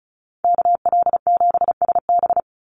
Morse code, heard as a keyed tone, about 34 wpm, 700 Hz. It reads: KL7SB
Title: Call Signs Repeated 2x - 35wpm